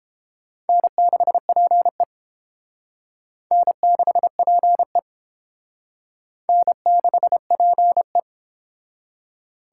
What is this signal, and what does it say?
Morse code, heard as a keyed tone, about 29 wpm, 700 Hz. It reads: N6PE N6PE N6PE